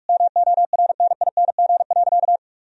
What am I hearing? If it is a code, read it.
Morse code, heard as a keyed tone, about 43 wpm, 700 Hz. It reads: MORNING.
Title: Sentences from Top 1000 Words - Review Part 3 - 45wpm